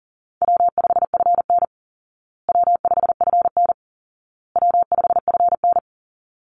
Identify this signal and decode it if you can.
Morse code, heard as a keyed tone, about 40 wpm, 700 Hz. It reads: W5FN W5FN W5FN
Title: Call Signs Repeated 2x - 40wpm